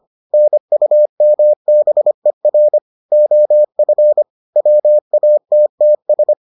Morse code, heard as a keyed tone, about 25 wpm, 600 Hz. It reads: NUMBER OF WATTS